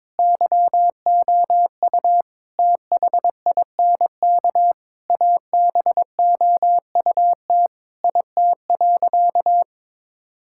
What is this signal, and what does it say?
Morse code, heard as a keyed tone, about 21 wpm, 700 Hz. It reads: YOU THINK ABOUT IT.
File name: Sentences with 7-Word Phrases - 22wpm